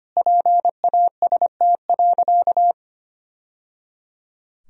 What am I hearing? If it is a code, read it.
Morse code, heard as a keyed tone, about 25 wpm, 700 Hz. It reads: PAST.
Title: Sentences from Top 1000 Words - Review Part 3 - 25wpm